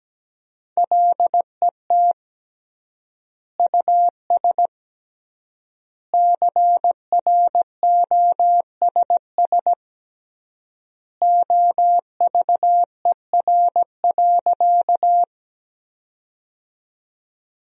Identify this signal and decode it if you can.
Morse code, heard as a keyed tone, about 17 wpm, 700 Hz. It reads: LET US CROSS OVER.
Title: 3x Word-Spacing - Sentences from Top 300 Words - 17wpm